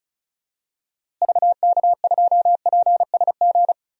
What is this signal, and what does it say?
Morse code, heard as a keyed tone, about 34 wpm, 700 Hz. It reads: VK2PSG
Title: DXCC-100 Call Signs - 6 Characters - Repeated 2x - 35wpm